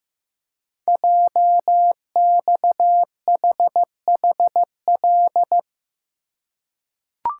Morse code, heard as a keyed tone, about 15 wpm, 700 Hz. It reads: JXHHL
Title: Code Groups - 5 Letters - 15wpm